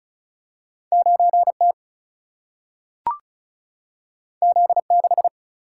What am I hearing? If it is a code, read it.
Morse code, heard as a keyed tone, about 36 wpm, 700 Hz. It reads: 9 T Z 6
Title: Single-Letter Single-Number 35wpm